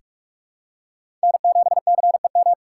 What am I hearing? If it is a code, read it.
Morse code, heard as a keyed tone, about 46 wpm, 700 Hz. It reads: N7CEM
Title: USA Call Signs - 5 Characters - 6x Character Spacing - 45wpm